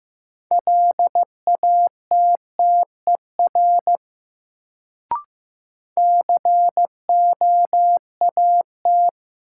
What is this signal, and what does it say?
Morse code, heard as a keyed tone, about 15 wpm, 700 Hz. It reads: LATTER COAT
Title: Sentences from Top 1600 Words - New Word Drill Alt - 15wpm